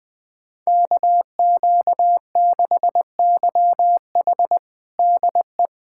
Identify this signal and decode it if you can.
Morse code, heard as a keyed tone, about 20 wpm, 700 Hz. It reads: KQ6YH DE
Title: USA Callsign DE Callsign 20wpm